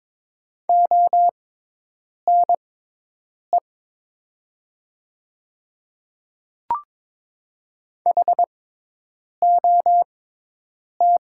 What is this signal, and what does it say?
Morse code, heard as a keyed tone, about 23 wpm, spaced as for 8 wpm, 700 Hz. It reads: ONE HOT
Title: 6x Character Spacing - Top 100 Words - 22wpm